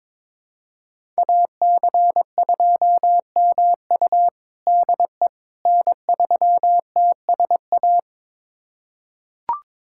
Morse code, heard as a keyed tone, about 22 wpm, 700 Hz. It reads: AC2MU DE N3TSA
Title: USA Callsign DE Callsign - 5 Characters - 22wpm